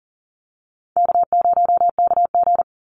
Morse code, heard as a keyed tone, about 40 wpm, 700 Hz. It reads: K0KG
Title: Easy Call Signs Repeated 2x - 40wpm